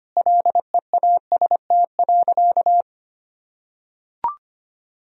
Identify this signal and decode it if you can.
Morse code, heard as a keyed tone, about 25 wpm, 700 Hz. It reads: LEAST.
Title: Sentences from Top 1000 Words - Review Part 2 - 25wpm